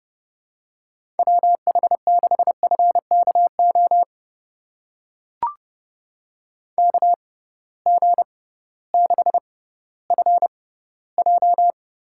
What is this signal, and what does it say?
Morse code, heard as a keyed tone, about 30 wpm, spaced as for 16 wpm, 700 Hz. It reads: WH6FKO KG6FJ
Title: USA Call Signs - 6 Characters - 6x Character Spacing - 30wpm